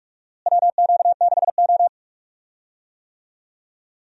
Morse code, heard as a keyed tone, about 46 wpm, 700 Hz. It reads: WQ6O
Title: USA Call Signs - 4 Characters - Repeated 3x - 45wpm